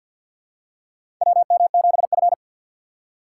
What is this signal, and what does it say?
Morse code, heard as a keyed tone, about 50 wpm, 700 Hz. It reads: WM7F
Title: USA Call Signs - 4 Characters - 3x Character Spacing - 50wpm